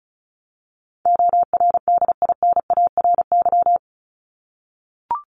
Morse code, heard as a keyed tone, about 34 wpm, 700 Hz. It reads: ORDINARY
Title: Top 1751-2000 Words 35wpm